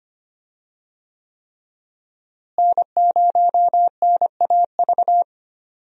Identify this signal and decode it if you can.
Morse code, heard as a keyed tone, about 25 wpm, 700 Hz. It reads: N0NAV
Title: USA Call Signs - 5 Characters - Repeated 3x - 25wpm